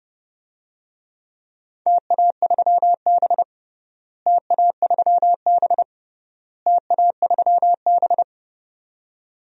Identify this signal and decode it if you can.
Morse code, heard as a keyed tone, about 30 wpm, 700 Hz. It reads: TA3B TA3B TA3B
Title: DXCC-100 Call Signs - 4 Characters - Repeated 3x - 30wpm